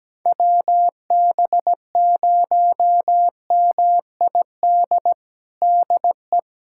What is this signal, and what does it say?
Morse code, heard as a keyed tone, about 17 wpm, 700 Hz. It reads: WB0MID DE
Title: USA Callsign DE Callsign - 6 Characters - 17wpm